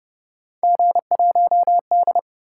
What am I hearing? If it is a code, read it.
Morse code, heard as a keyed tone, about 30 wpm, 700 Hz. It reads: G1D
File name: Speed-Racing - Three Letter-Number Combinations - 25wpm